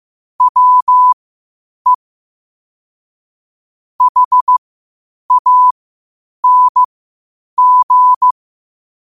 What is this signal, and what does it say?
Morse code, heard as a keyed tone, about 15 wpm, spaced as for 8 wpm, 1000 Hz. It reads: WE HANG
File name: qotdfiles / 8wpm